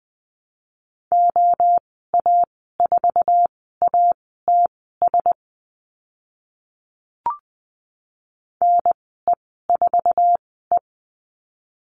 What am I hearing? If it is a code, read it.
Morse code, heard as a keyed tone, about 20 wpm, spaced as for 14 wpm, 700 Hz. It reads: OA4ATS NE4E
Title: 082 Lesson 14g - Number 2 - Call Signs 20wpm